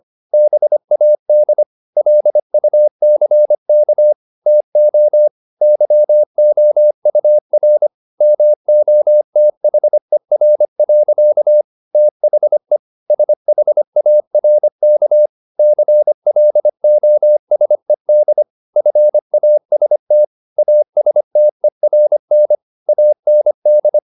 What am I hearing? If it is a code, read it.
Morse code, heard as a keyed tone, about 25 wpm, 600 Hz. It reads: BAD LUCK TO YOUR MOTHER. THE SHARK CLOSED FAST ASTERN AND